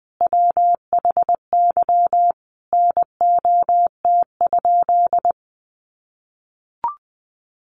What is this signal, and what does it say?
Morse code, heard as a keyed tone, about 20 wpm, 700 Hz. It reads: WHY NOT?
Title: Sentences from Top 600 Words 20wpm